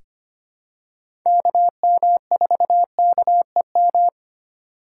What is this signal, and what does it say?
Morse code, heard as a keyed tone, about 25 wpm, 700 Hz. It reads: KM4KEM
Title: USA Call Signs - 3x Character Spacing - 25wpm